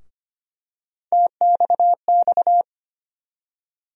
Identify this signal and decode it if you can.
Morse code, heard as a keyed tone, about 25 wpm, 700 Hz. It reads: TXX
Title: USA Call Sign Postfixes - 3 Letters 25wpm